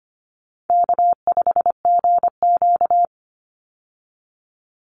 Morse code, heard as a keyed tone, about 25 wpm, 700 Hz. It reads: K5GQ
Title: Easy Call Signs Repeated 2x - 25wpm